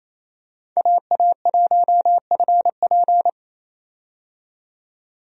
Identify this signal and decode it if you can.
Morse code, heard as a keyed tone, about 28 wpm, 700 Hz. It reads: AA1FP
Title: USA Call Signs - 5 Characters - 28wpm